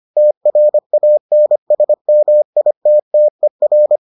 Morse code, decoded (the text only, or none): TRANSMITTER